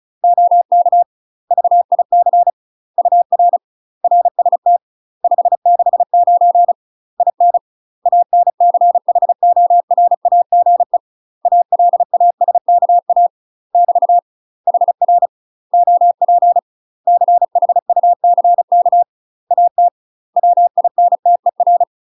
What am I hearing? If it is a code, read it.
Morse code, heard as a keyed tone, about 34 wpm, 700 Hz. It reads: OK VIC UR RST 569 IN ANCHORAGE ALASKA <BT> HR OP CHUCK AT WINTER
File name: UK MASTER QSOs TRAINING / 35WPM / UK MASTER QSOs 0021